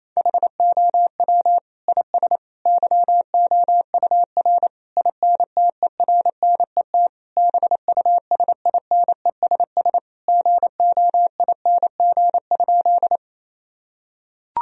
Morse code, heard as a keyed tone, about 27 wpm, 700 Hz. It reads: HOW IS YOUR INTERNET BUSINESS GOING?
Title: Sentences from Top 1000 Words - Review Part 4 - 28wpm